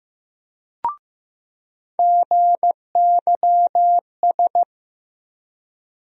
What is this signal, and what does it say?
Morse code, heard as a keyed tone, about 15 wpm, 700 Hz. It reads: GYS
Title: USA Call Sign Postfixes - 3 Letters 15wpm